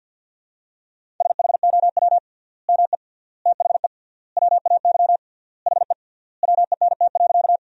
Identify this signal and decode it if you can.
Morse code, heard as a keyed tone, about 50 wpm, 700 Hz. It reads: SHOW ME THE WAY HE WENT.
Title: 3x Word-Spacing - Sentences from Top 200 Words - 50wpm